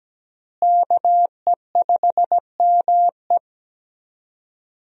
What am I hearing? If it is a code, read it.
Morse code, heard as a keyed tone, about 17 wpm, 700 Hz. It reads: KE5ME
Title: USA Call Signs - 5 Characters - Repeated 3x - 17wpm